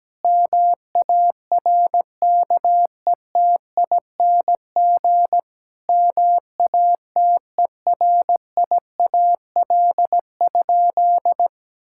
Morse code, heard as a keyed tone, about 17 wpm, 700 Hz. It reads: MARKETING MATERIAL?
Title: Sentences from Top 1000 Words - Review Part 5 - 17wpm